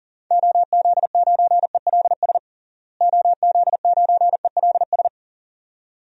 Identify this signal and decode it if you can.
Morse code, heard as a keyed tone, about 40 wpm, 700 Hz. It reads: OZ9ELS OZ9ELS
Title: DXCC-100 Call Signs - 6 Characters - Repeated 2x - 40wpm